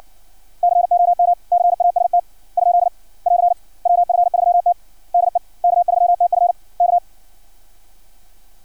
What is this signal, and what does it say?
Morse code, heard as a keyed tone, about 56 wpm, 700 Hz. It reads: COMCAST ? <AR> WF3T DE K3TF K
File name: archived / CRSnet